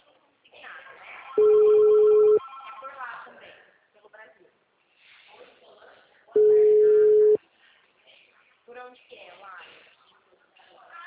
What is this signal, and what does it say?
Ringback tone (Europe, 425 Hz), with speech.